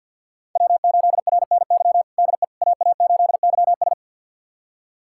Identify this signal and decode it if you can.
Morse code, heard as a keyed tone, about 48 wpm, 700 Hz. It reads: W8RNY DE AA8YS
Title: USA Callsign DE Callsign 50wpm